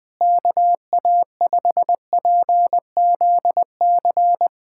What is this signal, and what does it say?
Morse code, heard as a keyed tone, about 20 wpm, 700 Hz. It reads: KA5PZC
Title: USA Callsign DE Callsign - 6 Characters - 20wpm